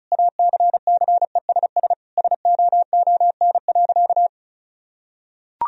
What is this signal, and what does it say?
Morse code, heard as a keyed tone, about 34 wpm, 700 Hz. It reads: ACCESS SOON.
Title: Sentences from Top 1000 Words - Review Part 4 - 35wpm